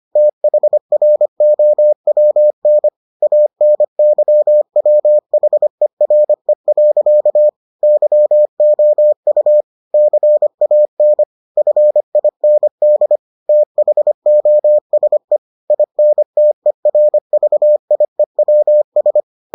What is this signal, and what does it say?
Morse code, heard as a keyed tone, about 25 wpm, 600 Hz. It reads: THROWN ANYWHERE. YOU CAN FIND THOSE INTERVIEWS